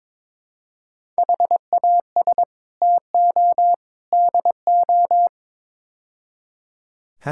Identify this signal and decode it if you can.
Morse code, heard as a keyed tone, about 22 wpm, 700 Hz. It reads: HAS TO DO
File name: practice / 22wpm / Build A Phrase - 5-Word Phrases - 22wpm